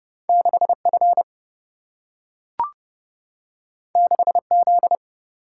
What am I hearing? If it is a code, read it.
Morse code, heard as a keyed tone, about 30 wpm, 700 Hz. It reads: 6F 6Z
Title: Single-Letter Single-Number 30wpm